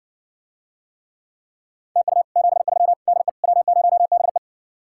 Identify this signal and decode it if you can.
Morse code, heard as a keyed tone, about 60 wpm, 700 Hz. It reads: TU 73 DE W0BE